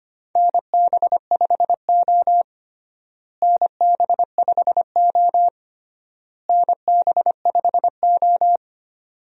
Morse code, heard as a keyed tone, about 25 wpm, 700 Hz. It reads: NB5O NB5O NB5O
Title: USA Call Signs - 4 Characters - Repeated 3x - 25wpm